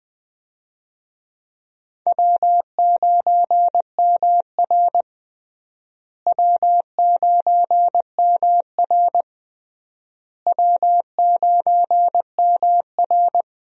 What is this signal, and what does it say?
Morse code, heard as a keyed tone, about 20 wpm, 700 Hz. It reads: W9MR W9MR W9MR
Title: USA Call Signs - 4 Characters - Repeated 3x - 20wpm